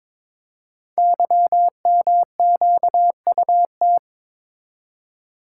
Morse code, heard as a keyed tone, about 21 wpm, 700 Hz. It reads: YMQUT
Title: Code Groups - 5 Letters - 22wpm